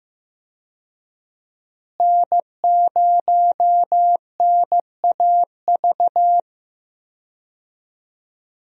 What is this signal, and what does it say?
Morse code, heard as a keyed tone, about 15 wpm, 700 Hz. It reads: N0NAV
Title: USA Call Signs - 5 Characters - Repeated 3x - 15wpm